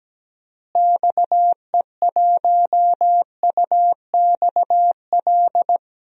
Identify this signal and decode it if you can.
Morse code, heard as a keyed tone, about 17 wpm, 700 Hz. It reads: XE1UXL
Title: DXCC-100 Call Signs - 6 Characters - Repeated 2x - 17wpm